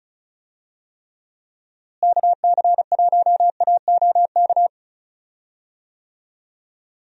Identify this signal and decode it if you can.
Morse code, heard as a keyed tone, about 36 wpm, 700 Hz. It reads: KC1AOK
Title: USA Call Signs - 6 Characters - Repeated 2x - 35wpm